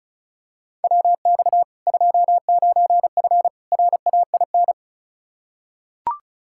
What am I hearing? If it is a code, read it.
Morse code, heard as a keyed tone, about 34 wpm, 700 Hz. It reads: WX 29F RAIN
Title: K1USN SST QSO Practice 35wpm